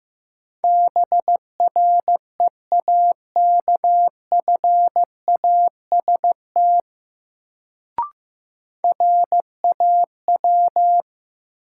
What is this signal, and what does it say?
Morse code, heard as a keyed tone, about 15 wpm, 700 Hz. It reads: BREAKFAST RAW
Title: Sentences from Top 1500 Words - New Word Drill Alt - 15wpm